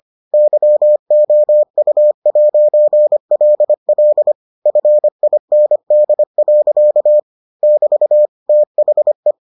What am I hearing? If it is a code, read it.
Morse code, heard as a keyed tone, about 25 wpm, 600 Hz. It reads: YOU'LL FIND. <BT> THE